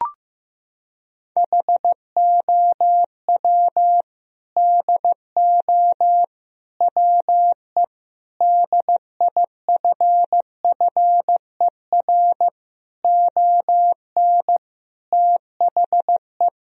Morse code, heard as a keyed tone, about 16 wpm, 700 Hz. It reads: HOW DO WE DIFFER ON THE
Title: Sentences from Top 1000 Words - Review Part 1 - 15wpm